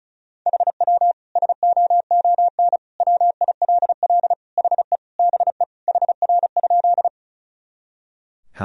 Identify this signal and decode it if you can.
Morse code, heard as a keyed tone, about 34 wpm, 700 Hz. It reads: HW SOON WILL HE BE HR?
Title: Sentences from Top 400 Words and Abbreviations 35wpm